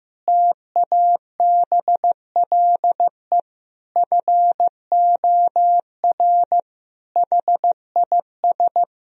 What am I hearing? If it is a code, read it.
Morse code, heard as a keyed tone, about 16 wpm, 700 Hz. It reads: TABLE FOR HIS
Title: Sentences from Top 1000 Words - Review Part 3 - 15wpm